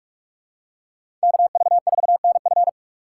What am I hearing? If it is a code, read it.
Morse code, heard as a keyed tone, about 46 wpm, 700 Hz. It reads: KV4NF
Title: USA Call Signs - 5 Characters - 2x Character Spacing - 45wpm